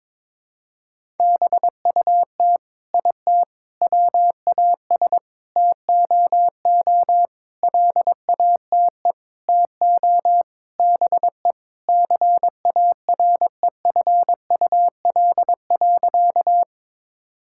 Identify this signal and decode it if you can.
Morse code, heard as a keyed tone, about 22 wpm, 700 Hz. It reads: BUT IT WAS TOO LATE TO BE CAREFUL.
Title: Sentences from Top 1600 Words 22wpm